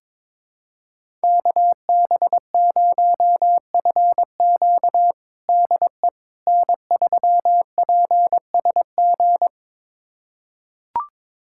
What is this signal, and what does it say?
Morse code, heard as a keyed tone, about 22 wpm, 700 Hz. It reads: KB0FQ DE N3PSG
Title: USA Callsign DE Callsign - 5 Characters - 22wpm